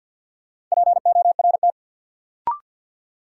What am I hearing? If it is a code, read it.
Morse code, heard as a keyed tone, about 52 wpm, 700 Hz. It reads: PORT
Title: Sentences from Top 1000 Words - Review Part 1 - New Word Drill - 50wpm